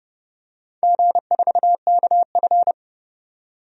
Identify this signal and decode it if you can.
Morse code, heard as a keyed tone, about 30 wpm, 700 Hz. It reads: G4KF
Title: DXCC-100 Call Signs - 4 Characters - 30wpm